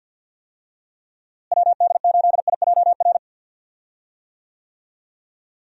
Morse code, heard as a keyed tone, about 48 wpm, 700 Hz. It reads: WD8IJR
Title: USA Call Signs - 6 Characters - Repeated 3x - 50wpm